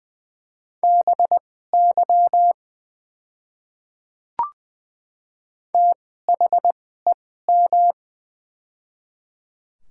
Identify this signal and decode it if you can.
Morse code, heard as a keyed tone, about 20 wpm, spaced as for 14 wpm, 700 Hz. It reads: BY THEM
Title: 2x Character Spacing - Top 100 Words - 20wpm